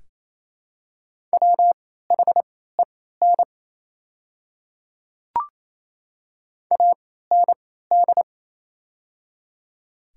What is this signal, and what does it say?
Morse code, heard as a keyed tone, about 28 wpm, spaced as for 16 wpm, 700 Hz. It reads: WHEN AND
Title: 3x Character Spacing - Top 200 Words - 28wpm